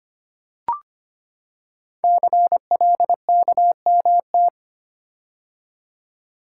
Morse code, heard as a keyed tone, about 25 wpm, 700 Hz. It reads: CLKMT
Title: Code Groups - 5 Letters - 25wpm